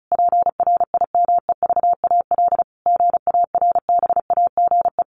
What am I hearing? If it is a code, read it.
Morse code, heard as a keyed tone, about 34 wpm, 700 Hz. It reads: PRIMEVAL GARBAGE
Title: A Sound Of Thunder 35wpm